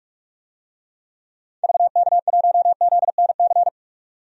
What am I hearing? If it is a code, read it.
Morse code, heard as a keyed tone, about 42 wpm, 700 Hz. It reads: VK1ZNC
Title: DXCC-100 Call Signs - 6 Characters - 45wpm